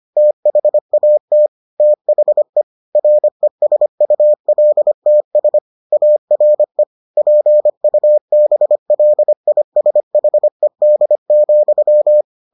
Morse code, heard as a keyed tone, about 25 wpm, 600 Hz. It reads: THAT THE RESULTS ARE PUBLISHED,